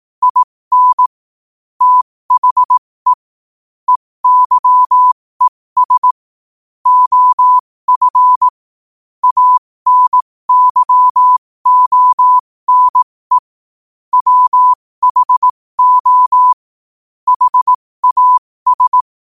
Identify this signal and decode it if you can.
Morse code, heard as a keyed tone, about 18 wpm, spaced as for 15 wpm, 1000 Hz. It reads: IN THE EYES OF ANYONE WHO HAS
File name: qotdfiles / 15wpm